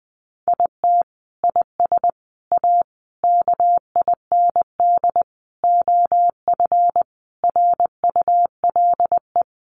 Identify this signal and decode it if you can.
Morse code, heard as a keyed tone, about 20 wpm, 700 Hz. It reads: IT IS A KIND OF RULE
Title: Sentences from Top 1500 Words 20wpm